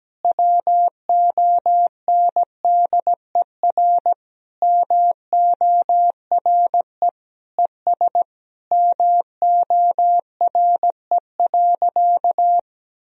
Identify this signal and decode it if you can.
Morse code, heard as a keyed tone, about 17 wpm, 700 Hz. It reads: WONDER MORE ES MORE.
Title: Sentences from Top 500 Words and Abbreviations 17wpm